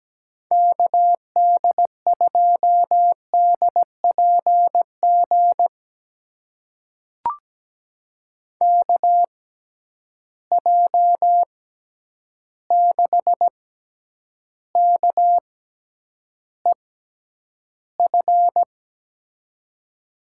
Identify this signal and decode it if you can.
Morse code, heard as a keyed tone, about 17 wpm, spaced as for 9 wpm, 700 Hz. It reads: KD2DPG KJ6KEF
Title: USA Call Signs - 6 Characters - 6x Character Spacing - 17wpm